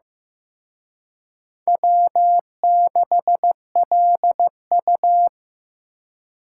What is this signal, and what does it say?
Morse code, heard as a keyed tone, about 18 wpm, 700 Hz. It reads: W6LU
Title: Speed Racing - USA 1x2 Call Signs 15wpm